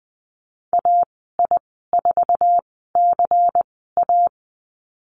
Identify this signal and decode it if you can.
Morse code, heard as a keyed tone, about 20 wpm, spaced as for 14 wpm, 700 Hz. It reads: AI4CA
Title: 140 Lesson 23g - Letter f - Call Signs 20wpm